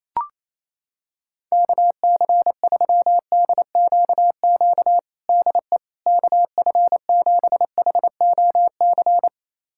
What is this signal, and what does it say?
Morse code, heard as a keyed tone, about 28 wpm, 700 Hz. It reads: KC3DQQ DE KF7HOC
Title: USA Callsign DE Callsign 28wpm